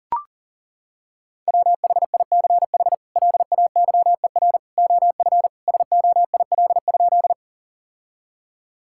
Morse code, heard as a keyed tone, about 40 wpm, 700 Hz. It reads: WHICH LAYER OF SOIL?
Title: Sentences from Top 1600 Words 40wpm